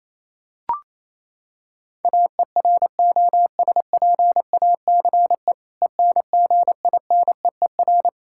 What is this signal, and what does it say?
Morse code, heard as a keyed tone, about 28 wpm, 700 Hz. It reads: AEROSPACE ENGINEER
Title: K1USN SST QSO Practice 28wpm